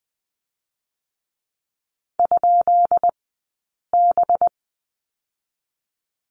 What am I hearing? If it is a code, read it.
Morse code, heard as a keyed tone, about 20 wpm, spaced as for 8 wpm, 700 Hz. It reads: ?B
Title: 204 Lesson 33d - Letter j - Two Characters 20wpm